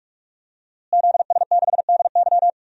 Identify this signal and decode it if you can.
Morse code, heard as a keyed tone, about 42 wpm, 700 Hz. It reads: ZS6DY
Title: DXCC-100 Call Signs - 5 Characters - Repeated 2x - 45wpm